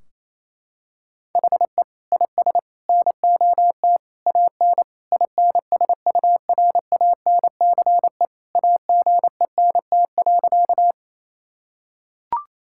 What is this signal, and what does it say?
Morse code, heard as a keyed tone, about 28 wpm, 700 Hz. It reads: HE IS NOT AN INSURANCE AGENT.
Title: Sentences from Top 1600 Words 28wpm